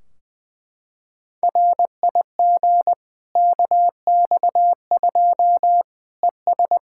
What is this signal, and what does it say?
Morse code, heard as a keyed tone, about 21 wpm, 700 Hz. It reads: RIG KX2 ES
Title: Practice QSO - Exchange 2 - Info - 20wpm